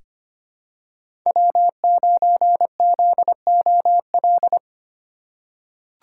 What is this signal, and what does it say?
Morse code, heard as a keyed tone, about 25 wpm, 700 Hz. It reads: W9ZOL
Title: USA Call Signs - 5 Characters - 4x Character Spacing - 25wpm